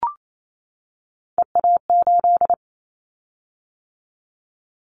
Morse code, heard as a keyed tone, about 28 wpm, 700 Hz. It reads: EA8
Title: Callsign Prefixes for Top 50 DXCC - 28wpm